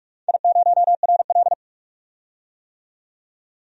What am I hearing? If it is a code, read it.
Morse code, heard as a keyed tone, about 41 wpm, 700 Hz. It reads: I0RL
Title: DXCC-100 Call Signs - 4 Characters - 45wpm